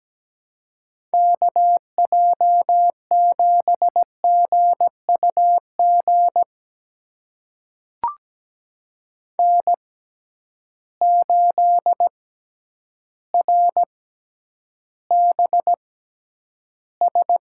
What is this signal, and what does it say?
Morse code, heard as a keyed tone, about 17 wpm, spaced as for 9 wpm, 700 Hz. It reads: KJ7GUG N8RBS
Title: USA Call Signs - 6x Character Spacing - 17wpm